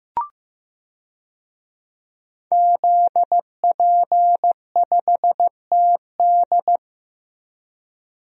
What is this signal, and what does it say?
Morse code, heard as a keyed tone, about 16 wpm, 700 Hz. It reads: ZP5TD
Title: DXCC-100 Call Signs - 5 Characters - 15wpm